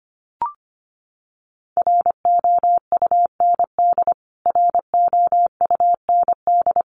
Morse code, heard as a keyed tone, about 25 wpm, 700 Hz. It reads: ROUND ROUND
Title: Top 300 Words Repeated 25wpm